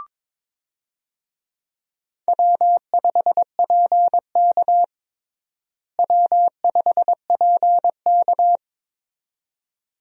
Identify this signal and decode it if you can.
Morse code, heard as a keyed tone, about 22 wpm, 700 Hz. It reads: W5PK W5PK
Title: USA Call Signs - 4 Characters - Repeated 2x - 22wpm